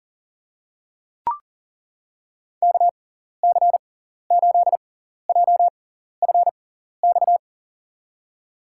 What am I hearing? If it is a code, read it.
Morse code, heard as a keyed tone, about 40 wpm, spaced as for 14 wpm, 700 Hz. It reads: KC8JFX
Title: USA Call Signs - 6 Characters - 6x Character Spacing - 40wpm